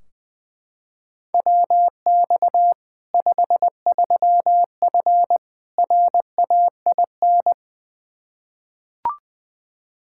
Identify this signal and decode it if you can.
Morse code, heard as a keyed tone, about 20 wpm, 700 Hz. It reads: WX 53F RAIN
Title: Practice QSO - Exchange 2 - Info - 20wpm